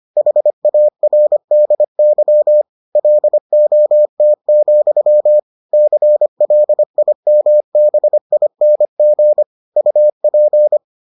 Morse code, heard as a keyed tone, about 25 wpm, 600 Hz. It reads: HARDY LOT, CLIMBING UP